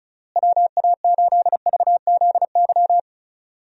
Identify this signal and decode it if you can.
Morse code, heard as a keyed tone, about 36 wpm, 700 Hz. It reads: WA8VZY
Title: USA Call Signs - 6 Characters - Repeated 3x - 35wpm